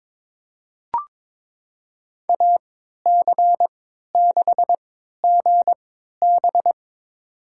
Morse code, heard as a keyed tone, about 23 wpm, spaced as for 13 wpm, 700 Hz. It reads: AC6GB
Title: USA Call Signs - 5 Characters - 3x Character Spacing - 22wpm